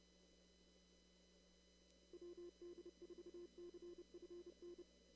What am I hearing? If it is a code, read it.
Morse code, heard as a keyed tone, about 30 wpm, 350 Hz. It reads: WD4CFN
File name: CW / kt5j wpx cw-op-k5pi